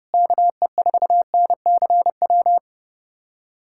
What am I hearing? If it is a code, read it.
Morse code, heard as a keyed tone, about 30 wpm, 700 Hz. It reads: KE4NCW
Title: USA Callsign DE Callsign - 6 Characters - 30wpm